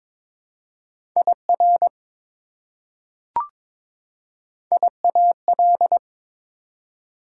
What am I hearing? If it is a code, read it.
Morse code, heard as a keyed tone, about 22 wpm, 700 Hz. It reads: IR IAL
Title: Prefixes and Suffixes 22wpm